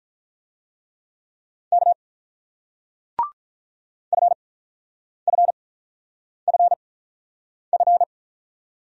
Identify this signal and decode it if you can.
Morse code, heard as a keyed tone, about 47 wpm, spaced as for 9 wpm, 690 Hz. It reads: K FFFF